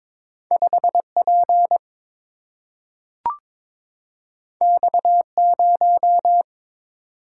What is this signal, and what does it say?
Morse code, heard as a keyed tone, about 22 wpm, 700 Hz. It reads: 5 P X 0
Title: Single-Letter Single-Number 22wpm